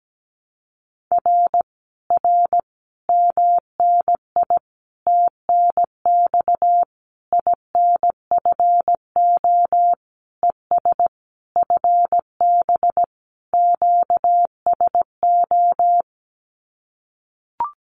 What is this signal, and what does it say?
Morse code, heard as a keyed tone, about 17 wpm, 700 Hz. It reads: R R MNI TNX INFO ES FB QSO
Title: Common QSO Phrases Encore 17wpm